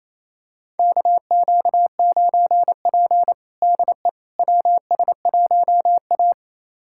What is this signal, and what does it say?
Morse code, heard as a keyed tone, about 28 wpm, 700 Hz. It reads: KQ9P DE WS1A
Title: USA Callsign DE Callsign - 4 Characters - 28wpm